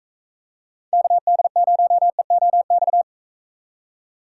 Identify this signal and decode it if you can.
Morse code, heard as a keyed tone, about 44 wpm, 700 Hz. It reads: KD0EOX